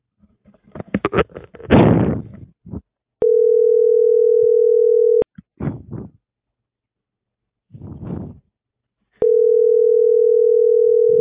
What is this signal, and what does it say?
Ringback tone (North America), 440 + 480 Hz.